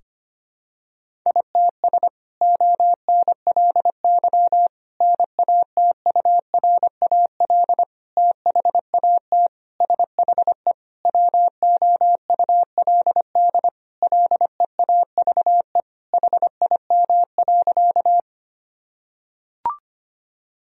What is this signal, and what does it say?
Morse code, heard as a keyed tone, about 25 wpm, 700 Hz. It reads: ITS ONLY NATURAL THAT SHE WOULD LEAVE HIM.
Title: Sentences from Top 1000 Words - Review Part 4 - 25wpm